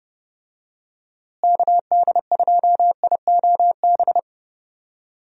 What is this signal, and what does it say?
Morse code, heard as a keyed tone, about 30 wpm, 700 Hz. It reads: KD2IOB
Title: USA Call Signs - 6 Characters - Repeated 3x - 30wpm